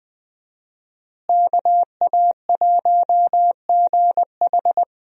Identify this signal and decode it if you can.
Morse code, heard as a keyed tone, about 20 wpm, 700 Hz. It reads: KA1GH
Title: USA Call Signs - 5 Characters - Repeated 3x - 20wpm